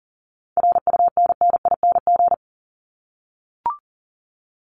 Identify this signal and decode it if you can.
Morse code, heard as a keyed tone, about 40 wpm, 700 Hz. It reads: RUNNING
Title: ING Suffix - Top 100 Words with Base Words 40wpm